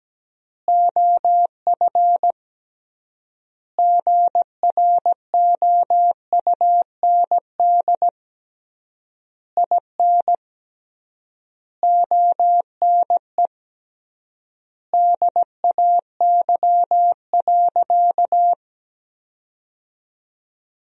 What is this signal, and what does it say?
Morse code, heard as a keyed tone, about 17 wpm, 700 Hz. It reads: OF GROUND IN ONE DAY.
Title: 3x Word-Spacing - Sentences from Top 500 Words - 17wpm